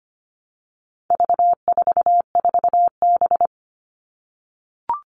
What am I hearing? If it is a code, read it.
Morse code, heard as a keyed tone, about 25 wpm, 700 Hz. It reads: V44B
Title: All Cut Numbers - 4-Digit 25wpm